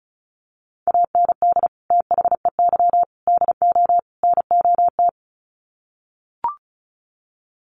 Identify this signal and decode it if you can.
Morse code, heard as a keyed tone, about 34 wpm, 700 Hz. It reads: AND THEY DO NOT
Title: Sets of 4 Words 35wpm